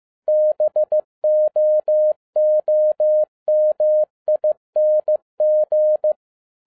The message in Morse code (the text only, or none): BOOMING